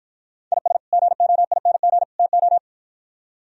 Morse code, heard as a keyed tone, about 50 wpm, 700 Hz. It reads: IS GOING TO